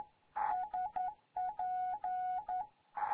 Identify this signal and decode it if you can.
Morse code, heard as a keyed tone, about 11 wpm, 740 Hz. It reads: SP
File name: Morze-HA2MN